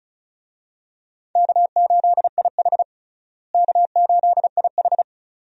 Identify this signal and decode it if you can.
Morse code, heard as a keyed tone, about 36 wpm, 700 Hz. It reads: K8IH K8IH
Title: USA Call Signs - 4 Characters - Repeated 2x - 35wpm